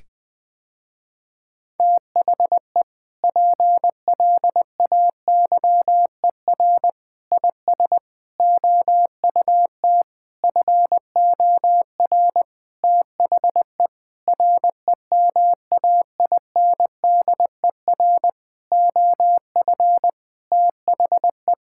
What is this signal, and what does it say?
Morse code, heard as a keyed tone, about 21 wpm, 700 Hz. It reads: THE PLAYER IS OUT FOR THE REMAINDER OF THE
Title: Sentences with 6-Word Phrases - Spoken Beginning - 20wpm